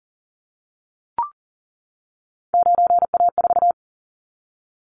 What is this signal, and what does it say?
Morse code, heard as a keyed tone, about 40 wpm, 700 Hz. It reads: 9A4
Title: Callsign Prefixes for Top 50 DXCC - 40wpm